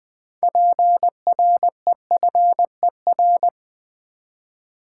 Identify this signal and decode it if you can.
Morse code, heard as a keyed tone, about 20 wpm, 700 Hz. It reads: PREFER
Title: Six Letter Words - 20wpm